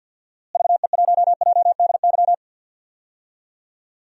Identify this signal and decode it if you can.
Morse code, heard as a keyed tone, about 52 wpm, 700 Hz. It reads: VE1JDY
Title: DXCC-100 Call Signs - 6 Characters - Repeated 2x - 50wpm